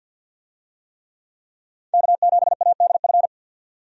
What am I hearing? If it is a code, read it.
Morse code, heard as a keyed tone, about 52 wpm, 700 Hz. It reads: K7ADF
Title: USA Call Signs - 5 Characters - 50wpm